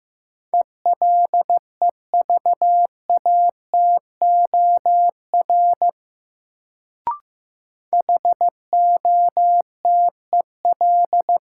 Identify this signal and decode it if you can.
Morse code, heard as a keyed tone, about 15 wpm, 700 Hz. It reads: ELEVATOR HOTEL
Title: Sentences from Top 1600 Words - New Word Drill Alt - 15wpm